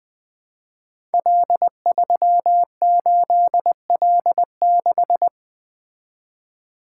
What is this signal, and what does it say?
Morse code, heard as a keyed tone, about 20 wpm, 700 Hz. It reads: L38L6
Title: Code Groups - 5 Letter:Numbers - 20wpm